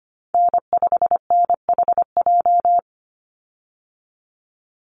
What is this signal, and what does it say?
Morse code, heard as a keyed tone, about 25 wpm, 700 Hz. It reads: N5NHJ
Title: Call Signs Repeated 2x - 25wpm